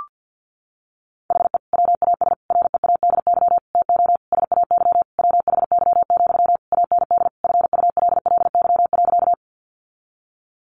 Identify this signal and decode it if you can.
Morse code, heard as a keyed tone, about 50 wpm, 700 Hz. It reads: HE WAS READY TO SAY WHY, AND FULLY.
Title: Sentences from Top 1300 Words 50wpm